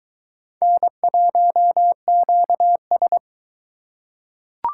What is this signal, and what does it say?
Morse code, heard as a keyed tone, about 21 wpm, 700 Hz. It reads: N1QS